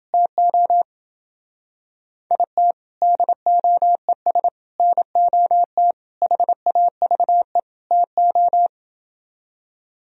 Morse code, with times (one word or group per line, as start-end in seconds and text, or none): 0.1-0.8 TO
2.3-2.7 IT
3.0-4.5 DOES
4.8-5.9 NOT
6.2-7.6 HAVE
7.9-8.7 TO